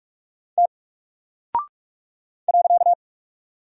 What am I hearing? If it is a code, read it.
Morse code, heard as a keyed tone, about 43 wpm, 700 Hz. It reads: T .
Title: Single Character (Letter:Number:Punctuation:Prosign) Alt 45wpm